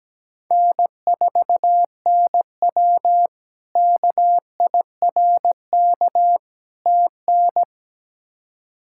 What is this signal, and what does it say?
Morse code, heard as a keyed tone, about 17 wpm, 700 Hz. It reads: N4NW KIRK TN
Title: K1USN SST QSO Practice - 17wpm